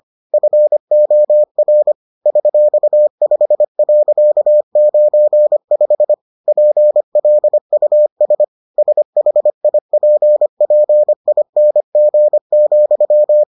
Morse code, heard as a keyed tone, about 25 wpm, 600 Hz. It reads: FOR $5.95 PLUS SHIPPING,